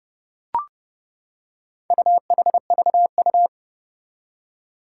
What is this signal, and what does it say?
Morse code, heard as a keyed tone, about 30 wpm, 700 Hz. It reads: UHVU
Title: Four Challenging Letter-Number Combinations 30wpm